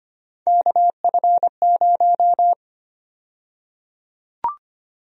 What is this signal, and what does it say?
Morse code, heard as a keyed tone, about 25 wpm, 700 Hz. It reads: KF0
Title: USA Call Sign Prefixes - 25wpm